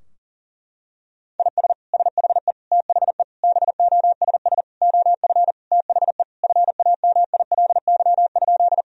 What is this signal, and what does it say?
Morse code, heard as a keyed tone, about 40 wpm, 700 Hz. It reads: IS SHE THE BOSS OF THE FAMILY?
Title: Sentences from Top 1000 Words - Review Part 4 - 40wpm